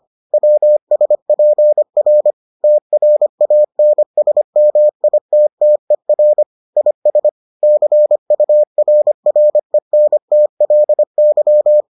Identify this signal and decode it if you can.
Morse code, heard as a keyed tone, about 25 wpm, 600 Hz. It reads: WSPR TRANSMITTER IS CURRENTLY